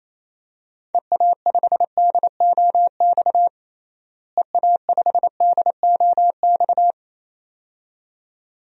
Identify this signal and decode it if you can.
Morse code, heard as a keyed tone, about 28 wpm, 700 Hz. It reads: EA5DOX EA5DOX
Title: DXCC-100 Call Signs - 6 Characters - Repeated 2x - 28wpm